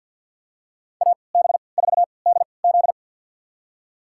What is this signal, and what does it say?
Morse code, heard as a keyed tone, about 52 wpm, spaced as for 29 wpm, 700 Hz. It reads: AB4DZ
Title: USA Call Signs - 3x Character Spacing - 50wpm